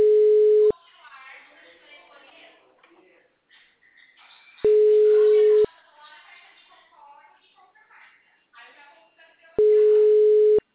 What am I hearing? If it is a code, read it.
Ringback tone (Europe, 425 Hz), with speech.
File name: out- -2009- - - .294